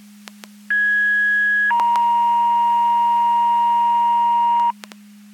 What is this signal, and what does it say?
Two-tone page, heard as a one-second tone, then a three-second tone, about 1650 then 940 Hz. It is heard from 0.7 to 4.7 s.
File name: Portage 154.385 Paging